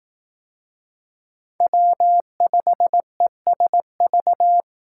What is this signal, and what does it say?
Morse code, heard as a keyed tone, about 18 wpm, 700 Hz. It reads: W5ESV
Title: Speed Racing - USA Call Signs 15wpm